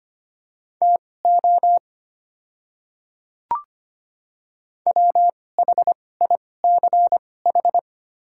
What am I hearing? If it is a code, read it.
Morse code, heard as a keyed tone, about 25 wpm, spaced as for 18 wpm, 700 Hz. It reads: TO WHICH
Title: 2x Character Spacing - Top 100 Words - 25wpm